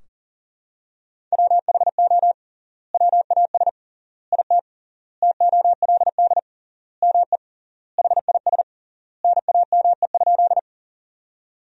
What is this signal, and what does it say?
Morse code, heard as a keyed tone, about 40 wpm, 700 Hz. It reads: WHO WAS IT TOLD ME HIS NAME?
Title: 3x Word-Spacing - Sentences from Top 400 Words - 40wpm